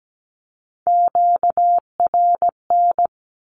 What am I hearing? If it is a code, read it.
Morse code, heard as a keyed tone, about 17 wpm, 700 Hz. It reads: QRN
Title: Q-Codes Repeated 3x 17wpm